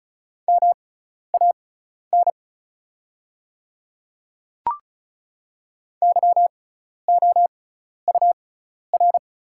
Morse code, heard as a keyed tone, about 36 wpm, spaced as for 12 wpm, 700 Hz. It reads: MAN YOUR
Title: 6x Character Spacing - Top 200 Words - 35wpm